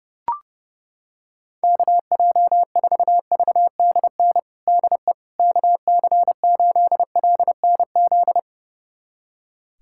Morse code, heard as a keyed tone, about 30 wpm, 700 Hz. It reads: KJ4VDN DE KC8LNZ
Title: USA Callsign DE Callsign - 6 Characters - 30wpm